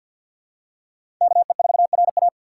Morse code, heard as a keyed tone, about 47 wpm, 700 Hz. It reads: KE4RA
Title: USA Call Signs - 4x Character Spacing - 50wpm